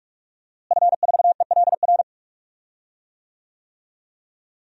Morse code, heard as a keyed tone, about 46 wpm, 700 Hz. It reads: F4ELR
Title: DXCC-100 Call Signs - 5 Characters - Repeated 3x - 45wpm